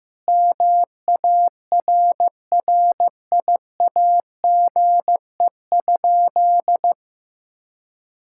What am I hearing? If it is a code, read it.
Morse code, heard as a keyed tone, about 15 wpm, 700 Hz. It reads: MARRIAGE?
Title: Sentences from Top 1000 Words - Review Part 5 - 15wpm